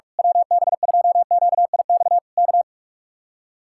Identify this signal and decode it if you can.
Morse code, heard as a keyed tone, about 45 wpm, 700 Hz. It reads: WB2QIX K
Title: Speed-Racing Common QSO Phrases Encore - 45wpm